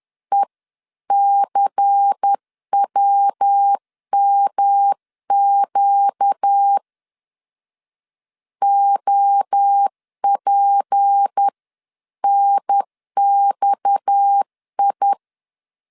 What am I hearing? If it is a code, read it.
Morse code, heard as a keyed tone, about 10 wpm, 800 Hz. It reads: ECWMQ OPNXI
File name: CW / DET / 1 6 1500C S C 10wpm 4252